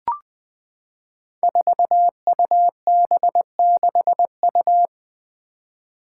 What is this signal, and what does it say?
Morse code, heard as a keyed tone, about 20 wpm, 700 Hz. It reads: 4UB6U
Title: Five Challenging Letter-Number Combinations 20wpm